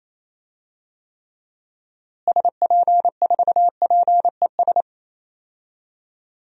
Morse code, heard as a keyed tone, about 28 wpm, 700 Hz. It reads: SP4PES
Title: DXCC-100 Call Signs - 6 Characters - Repeated 2x - 28wpm